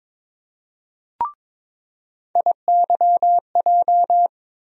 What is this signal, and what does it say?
Morse code, heard as a keyed tone, about 22 wpm, 700 Hz. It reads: IYJ
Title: USA Call Sign Postfixes - 3 Letters 22wpm